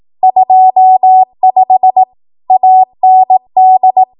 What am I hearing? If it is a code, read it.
Morse code, heard as a keyed tone, about 18 wpm, 750 Hz. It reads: 25 AND
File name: Archive / 18 WPM / 18WPM